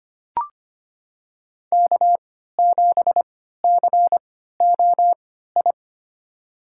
Morse code, heard as a keyed tone, about 25 wpm, spaced as for 14 wpm, 700 Hz. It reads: K7COI
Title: USA Call Signs - 5 Characters - 3x Character Spacing - 25wpm